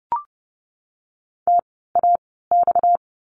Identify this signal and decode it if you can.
Morse code, heard as a keyed tone, about 30 wpm, spaced as for 17 wpm, 700 Hz. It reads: TAX
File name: course / 30:10wpm / 230 Lesson 37f - Letter x - Words 30wpm